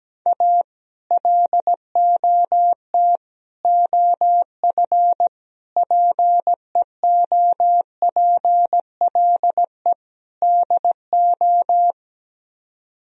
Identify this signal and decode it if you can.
Morse code, heard as a keyed tone, about 17 wpm, 700 Hz. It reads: A LOT OF PEOPLE DO
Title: Speed-Racing - Sets of 5 Words 17wpm